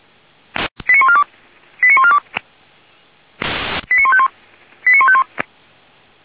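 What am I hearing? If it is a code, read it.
Five-tone selective call (ZVEI), heard as quick runs of five tones, tones of about 70 ms. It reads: 89142 89142 89151 89151
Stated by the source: Five-tone ZVEI selective call received on an SDR.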